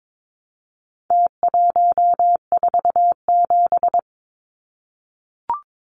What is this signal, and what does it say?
Morse code, heard as a keyed tone, about 22 wpm, 700 Hz. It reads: T147
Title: Numbers - Contest Style - 4-Digit 22wpm